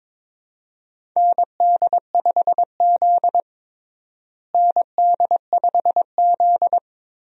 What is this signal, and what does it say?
Morse code, heard as a keyed tone, about 22 wpm, 700 Hz. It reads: ND5Z ND5Z
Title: USA Call Signs - 4 Characters - Repeated 2x - 22wpm